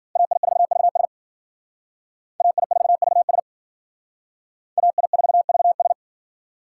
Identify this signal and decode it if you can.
Morse code, heard as a keyed tone, about 52 wpm, 700 Hz. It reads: AI4VS AI4VS AI4VS